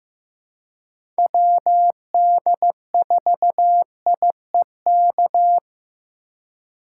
Morse code, heard as a keyed tone, about 15 wpm, 700 Hz. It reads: WD4IEK
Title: USA Call Signs - 6 Characters - 4x Character Spacing - 15wpm